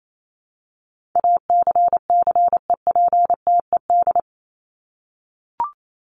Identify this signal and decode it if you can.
Morse code, heard as a keyed tone, about 28 wpm, 700 Hz. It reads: ACCEPTED
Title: ED Suffix - Top 100 Words with Base Words 28wpm